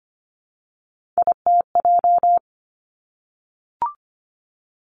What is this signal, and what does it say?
Morse code, heard as a keyed tone, about 25 wpm, 700 Hz. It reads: ITJ
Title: Three Challenging Letter-Number Combinations 25wpm